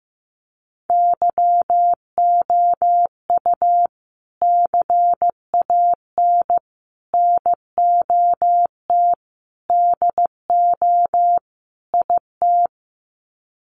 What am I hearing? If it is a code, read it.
Morse code, heard as a keyed tone, about 15 wpm, 700 Hz. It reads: YOU CAN NOT DO IT
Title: Sets of 5 Words Encore2 15wpm